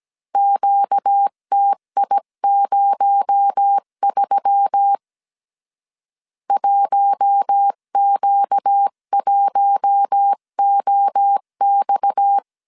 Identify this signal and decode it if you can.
Morse code, heard as a keyed tone, about 17 wpm, 800 Hz. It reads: QTI03 1Q1OX
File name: CW / DET / 1 6 1500C S C 16wpm 2733